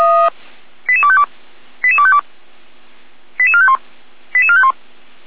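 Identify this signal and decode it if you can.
Five-tone selective call (ZVEI), heard as quick runs of five tones, tones of about 70 ms. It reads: 88252 88252 88451 88451
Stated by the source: ZVEI five-tone selective call received with an SDR, Feuerwehr Orlamünde.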